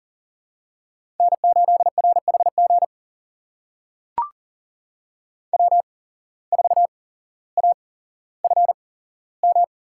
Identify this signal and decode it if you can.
Morse code, heard as a keyed tone, about 40 wpm, spaced as for 17 wpm, 700 Hz. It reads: N8RHG W4AFM
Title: USA Call Signs - 5 Characters - 8x Character Spacing - 40wpm